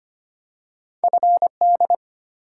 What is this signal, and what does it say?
Morse code, heard as a keyed tone, about 25 wpm, 700 Hz. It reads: F D
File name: Two Letter-Number Combinations - 25wpm